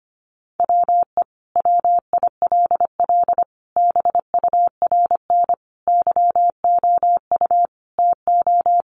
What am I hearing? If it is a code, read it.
Morse code, heard as a keyed tone, about 25 wpm, 700 Hz. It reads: WE WILL BURN YOU TO
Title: Sentences from Top 1300 Words 25wpm